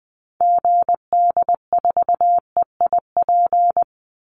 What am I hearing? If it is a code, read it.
Morse code, heard as a keyed tone, about 20 wpm, 700 Hz. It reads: GD4EIP
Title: Call Signs Repeated 3x - 20wpm